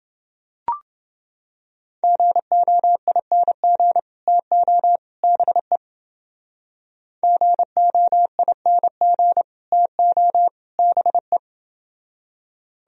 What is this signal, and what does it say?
Morse code, heard as a keyed tone, about 29 wpm, 700 Hz. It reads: GOING TO BE GOING TO BE